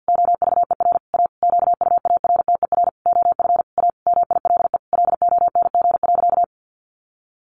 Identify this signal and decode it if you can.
Morse code, heard as a keyed tone, about 50 wpm, 700 Hz. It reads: OVER A QUARTER OF A MILE LONG.
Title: Sentences from Top 1300 Words 50wpm